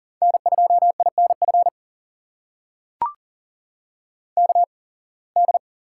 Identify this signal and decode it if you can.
Morse code, heard as a keyed tone, about 40 wpm, spaced as for 26 wpm, 700 Hz. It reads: N2INF KD
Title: USA Call Signs - 8x Character Spacing - 40wpm